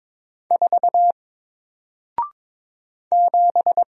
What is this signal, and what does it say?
Morse code, heard as a keyed tone, about 22 wpm, 700 Hz. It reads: 4 7
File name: Single-Letter-Number-22wpm